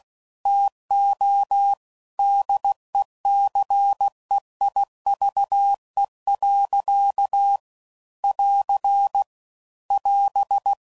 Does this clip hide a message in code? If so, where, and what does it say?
Yes: Morse code from 0.5 to 10.7 s, heard as a keyed tone, about 16 wpm, 800 Hz. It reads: TO DECEIVE. <AR> <AS>